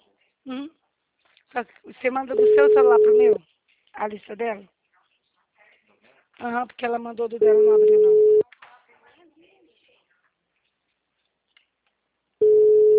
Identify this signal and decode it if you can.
Ringback tone (Europe, 425 Hz), with speech.